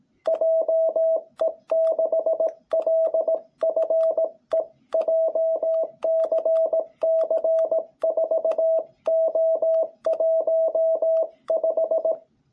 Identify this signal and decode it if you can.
Morse code, heard as a keyed tone, about 18 wpm, 660 Hz. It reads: JE6LFE JCC4O15
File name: jh8cbh / 2021cw